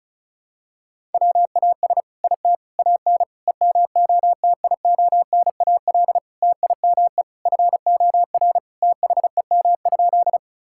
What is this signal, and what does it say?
Morse code, heard as a keyed tone, about 36 wpm, 700 Hz. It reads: WAS IT AN EMOTIONAL TIME FOR THEM?
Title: Sentences from Top 1000 Words - Review Part 5 - 35wpm